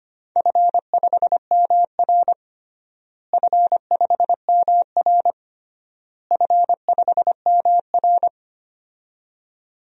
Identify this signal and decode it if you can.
Morse code, heard as a keyed tone, about 25 wpm, 700 Hz. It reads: F5MR F5MR F5MR
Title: DXCC-100 Call Signs - 4 Characters - Repeated 3x - 25wpm